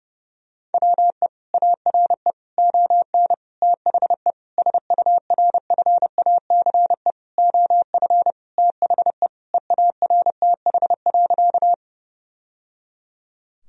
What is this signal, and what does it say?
Morse code, heard as a keyed tone, about 30 wpm, 700 Hz. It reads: WE ARE ON THE SURFACE OF THE EARTH.
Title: Sentences from Top 1000 Words - Review Part 3 - 30wpm